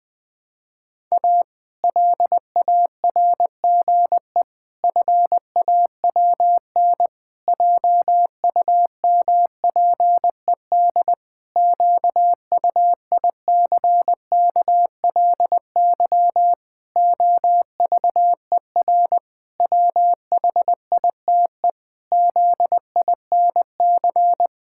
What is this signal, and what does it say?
Morse code, heard as a keyed tone, about 20 wpm, 700 Hz. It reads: A LARGE FAWN JUMPED QUICKLY OVER WHITE ZINC
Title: Pangram Challenge v2 - 20wpm